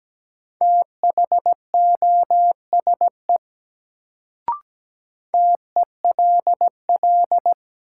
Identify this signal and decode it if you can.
Morse code, heard as a keyed tone, about 17 wpm, 700 Hz. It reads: THOSE TELL
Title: Sets of 2 Words - New Word Drill - 17wpm